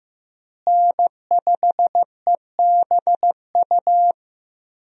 Morse code, heard as a keyed tone, about 15 wpm, 700 Hz. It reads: N5EBU
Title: USA Call Signs - 5 Characters - Repeated 3x - 15wpm